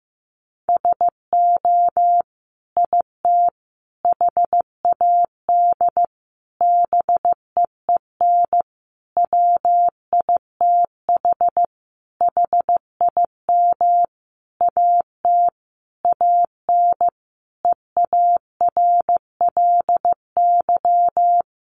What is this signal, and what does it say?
Morse code, heard as a keyed tone, about 15 wpm, 700 Hz. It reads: SO IT HAD BEEN WITH HIM AT AN EARLY
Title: Sentences from Top 1200 Words 15wpm